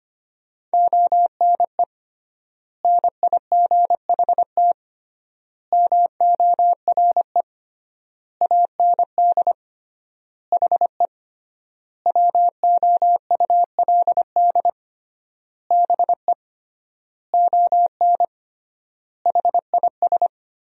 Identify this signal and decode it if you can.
Morse code, heard as a keyed tone, about 25 wpm, 700 Hz. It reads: ONE NIGHT MORE AND HE WOULD BE ON HIS
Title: 3x Word-Spacing - Sentences from Top 300 Words - 25wpm